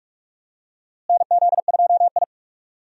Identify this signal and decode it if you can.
Morse code, heard as a keyed tone, about 46 wpm, 700 Hz. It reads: NZ2I
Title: USA Call Signs - 4 Characters - Repeated 2x - 45wpm